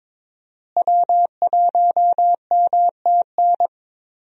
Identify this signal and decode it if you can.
Morse code, heard as a keyed tone, about 22 wpm, 700 Hz. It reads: W1MTN
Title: USA Call Signs - 5 Characters - Repeated 3x - 22wpm